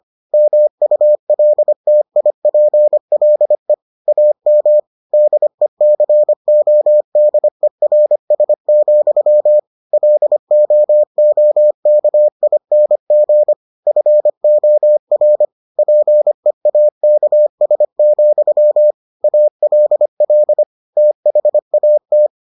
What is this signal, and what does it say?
Morse code, heard as a keyed tone, about 25 wpm, 600 Hz. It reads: MULTIPLE AM DECODERS, LOOKING FOR PEAKS, ALL THAT